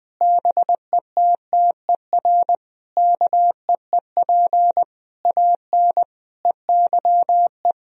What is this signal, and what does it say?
Morse code, heard as a keyed tone, about 20 wpm, 700 Hz. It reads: BETTER KEEP AN EYE
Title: Sentences from Top 1000 Words - Review Part 1 - 20wpm